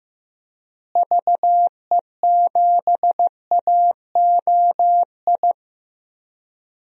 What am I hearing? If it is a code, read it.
Morse code, heard as a keyed tone, about 15 wpm, 700 Hz. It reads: VE7AOI
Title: DXCC-100 Call Signs - 6 Characters - Repeated 3x - 15wpm